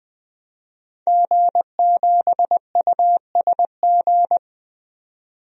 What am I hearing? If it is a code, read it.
Morse code, heard as a keyed tone, about 21 wpm, 700 Hz. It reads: G7USG
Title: DXCC-100 Call Signs - 5 Characters - 20wpm